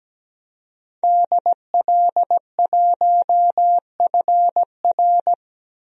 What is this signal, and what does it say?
Morse code, heard as a keyed tone, about 17 wpm, 700 Hz. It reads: DL1FR
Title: DXCC-100 Call Signs - 5 Characters - Repeated 2x - 17wpm